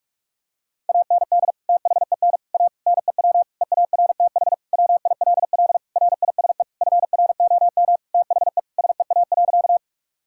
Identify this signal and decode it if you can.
Morse code, heard as a keyed tone, about 41 wpm, 700 Hz. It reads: AND THEN A NEW EARTH WILL RISE FROM THE SEA.
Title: Sentences from Top 1000 Words - Review Part 6 - 45wpm